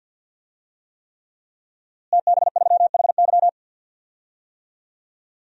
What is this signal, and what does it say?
Morse code, heard as a keyed tone, about 48 wpm, 700 Hz. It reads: TB3HY
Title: DXCC-100 Call Signs - 5 Characters - Repeated 3x - 50wpm